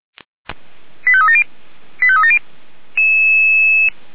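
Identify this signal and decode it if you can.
Five-tone selective call (ZVEI), heard as quick runs of five tones, tones of about 70 ms. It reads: 85370 85370
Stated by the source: ZVEI five-tone call received by SDR, archived by Feuerwehr Orlamünde fire brigade.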